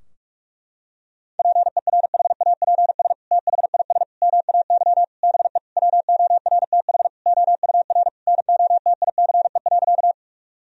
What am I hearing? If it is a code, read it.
Morse code, heard as a keyed tone, about 46 wpm, 700 Hz. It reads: PERHAPS THIS MAY BE WORTH OUR NOTICE.
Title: Sentences from Top 1000 Words - Review Part 2 - 45wpm